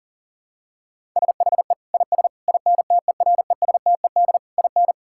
Morse code, heard as a keyed tone, about 37 wpm, 700 Hz. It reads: SHE IS INTERESTED IN
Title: Sentences from Top 1000 Words - Review Part 6 - 40wpm